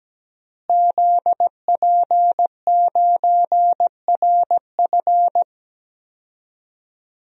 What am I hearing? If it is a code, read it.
Morse code, heard as a keyed tone, about 17 wpm, 700 Hz. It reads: ZP9RF
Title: DXCC-100 Call Signs - 5 Characters - Repeated 3x - 17wpm